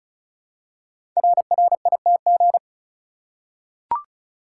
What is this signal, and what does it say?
Morse code, heard as a keyed tone, about 36 wpm, 700 Hz. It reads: RRITG
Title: Code Groups - 5 Letter:Numbers - 35wpm